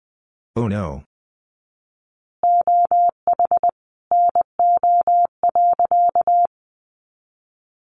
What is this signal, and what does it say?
Morse code, heard as a keyed tone, about 20 wpm, 700 Hz. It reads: OH NO.
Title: Additional Easy Phrases 20wpm